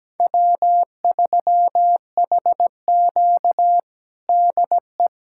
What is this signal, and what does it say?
Morse code, heard as a keyed tone, about 17 wpm, 700 Hz. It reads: W3HQ DE
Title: USA Callsign DE Callsign - 4 Characters - 17wpm